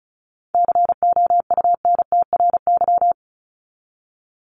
Morse code, heard as a keyed tone, about 34 wpm, 700 Hz. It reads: COUNTRY
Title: ES:S Suffix - Top 100 Words with Base Words 35wpm